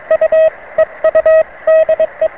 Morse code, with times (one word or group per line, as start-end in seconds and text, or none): 0.1-0.5 U
0.8-1.4 EU
1.7-2.3 DE